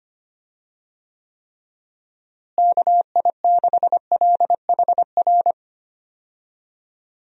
Morse code, heard as a keyed tone, about 25 wpm, 700 Hz. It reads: KI6LHR
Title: USA Call Signs - 6 Characters - Repeated 3x - 25wpm